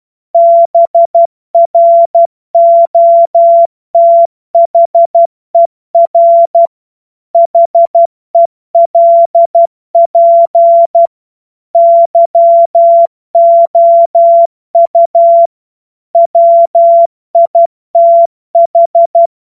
Morse code, heard as a keyed tone, about 12 wpm, 680 Hz. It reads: BROTHER HELP YOU WITH
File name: Morsekurs / Witze